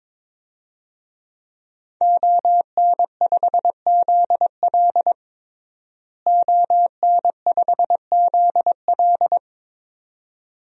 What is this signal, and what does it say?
Morse code, heard as a keyed tone, about 22 wpm, 700 Hz. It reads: ON5ZL ON5ZL
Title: DXCC-100 Call Signs - 5 Characters - Repeated 2x - 22wpm